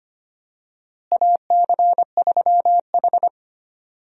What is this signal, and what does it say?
Morse code, heard as a keyed tone, about 25 wpm, 700 Hz. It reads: AC3H
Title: USA Call Signs - 4 Characters - Repeated 2x - 25wpm